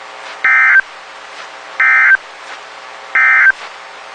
Emergency Alert System (SAME) tones, heard as harsh buzzing bursts. It is heard from 0.3 to 3.6 s.
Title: WXK91 NOAA Weather Radio